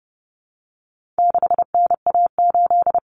Morse code, heard as a keyed tone, about 30 wpm, 700 Hz. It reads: 6NA8
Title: Common Cut Numbers - 4-Digit 30wpm